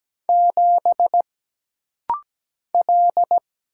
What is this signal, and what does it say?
Morse code, heard as a keyed tone, about 17 wpm, 700 Hz. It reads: 7 L
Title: Single Letter-Number - Rapid-Fire 17wpm